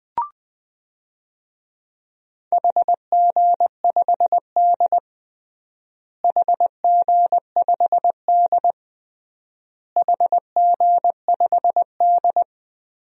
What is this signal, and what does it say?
Morse code, heard as a keyed tone, about 20 wpm, 700 Hz. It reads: HG5D HG5D HG5D
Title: DXCC-100 Call Signs - 4 Characters - Repeated 3x - 20wpm